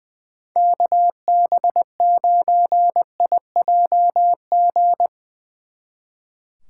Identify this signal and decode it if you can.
Morse code, heard as a keyed tone, about 22 wpm, 700 Hz. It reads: KB9IJG
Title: Speed Racing - USA 2x3 Call Signs 20wpm